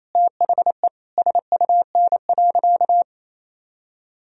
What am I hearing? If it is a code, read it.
Morse code, heard as a keyed tone, about 28 wpm, 700 Hz. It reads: THE SUN.
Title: Sentences from Top 1000 Words - Review Part 1 - 28wpm